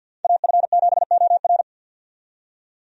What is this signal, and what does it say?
Morse code, heard as a keyed tone, about 51 wpm, 700 Hz. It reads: AF7OR
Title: USA Call Signs - 5 Characters - 50wpm